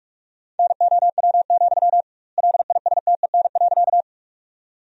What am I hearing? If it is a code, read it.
Morse code, heard as a keyed tone, about 41 wpm, 700 Hz. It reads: NOW, LISTEN.
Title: 2x Word-Spacing - Sentences from Top 400 Words - 45wpm